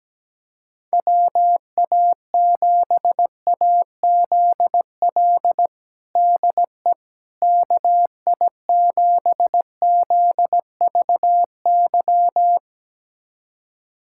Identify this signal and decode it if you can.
Morse code, heard as a keyed tone, about 17 wpm, 700 Hz. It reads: WA7AZL DE KI7ZVY
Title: USA Callsign DE Callsign - 6 Characters - 17wpm